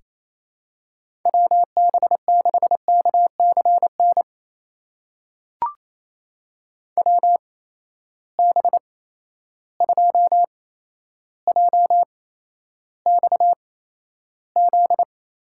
Morse code, heard as a keyed tone, about 28 wpm, spaced as for 12 wpm, 700 Hz. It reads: WB6KCN WB2JXZ
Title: USA Call Signs - 6 Characters - 8x Character Spacing - 28wpm